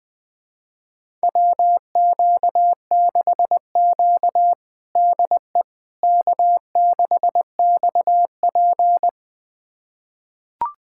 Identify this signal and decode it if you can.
Morse code, heard as a keyed tone, about 21 wpm, 700 Hz. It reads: WQ6Q DE K6XP
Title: USA Callsign DE Callsign - 4 Characters - 20wpm